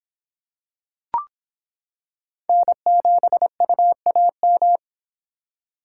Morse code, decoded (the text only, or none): N7UAM